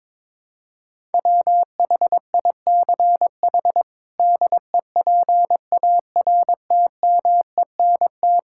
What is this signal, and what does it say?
Morse code, heard as a keyed tone, about 22 wpm, 700 Hz. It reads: WHICH DEPARTMENT
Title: Sentences from Top 1000 Words - Review Part 5 - 22wpm